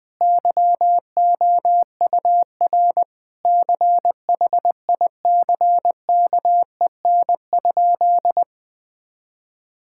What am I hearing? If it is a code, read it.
Morse code, heard as a keyed tone, about 20 wpm, 700 Hz. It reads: YOUR CHICKEN?
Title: Sentences from Top 1000 Words - Review Part 5 - 20wpm